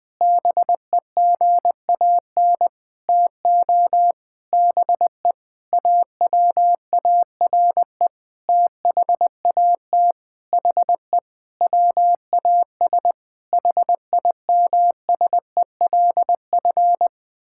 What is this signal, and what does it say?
Morse code, heard as a keyed tone, about 21 wpm, 700 Hz. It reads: BEGAN TO BE AWARE THAT HE WAS HIMSELF
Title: Sentences from Top 1600 Words 20wpm